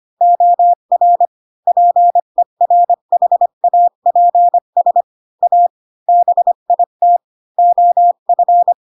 Morse code, heard as a keyed tone, about 25 wpm, 700 Hz. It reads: OR PERHAPS A BIT OF
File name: Pinocchio / Pinocchio ch5 20 25 ik1hge